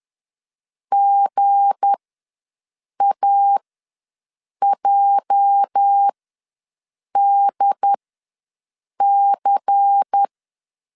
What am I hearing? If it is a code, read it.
Morse code, heard as a keyed tone, about 10 wpm, spaced as for 6 wpm, 800 Hz. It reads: GAJDC
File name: LEZIONI / 8 YJ45 / 8 YJ ws 10wpm 4043